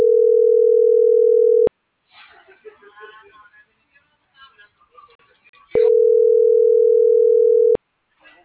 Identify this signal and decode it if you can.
Ringback tone (North America), 440 + 480 Hz.
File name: CENTRAL MAERSK / out sandovalmsk - .64774